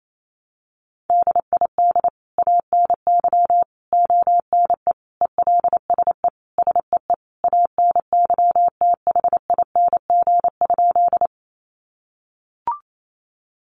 Morse code, heard as a keyed tone, about 28 wpm, 700 Hz. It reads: DID ANY ONE ELSE SEE ANYTHING?
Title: Sentences from Top 1000 Words 28wpm